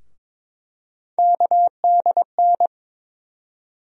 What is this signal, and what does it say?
Morse code, heard as a keyed tone, about 22 wpm, 700 Hz. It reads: KDN
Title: USA Call Sign Postfixes - 3 Letters 22wpm